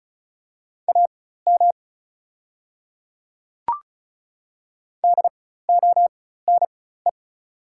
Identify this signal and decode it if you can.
Morse code, heard as a keyed tone, about 36 wpm, spaced as for 16 wpm, 690 Hz. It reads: AM DONE
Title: 4x Character Spacing - Top 500 Words - 35wpm